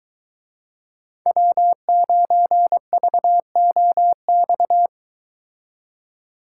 Morse code, heard as a keyed tone, about 23 wpm, 700 Hz. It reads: W9VOX